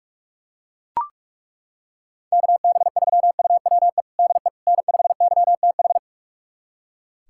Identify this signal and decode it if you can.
Morse code, heard as a keyed tone, about 41 wpm, 700 Hz. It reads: KB3UWE DE N5YTH
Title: USA Callsign DE Callsign 45wpm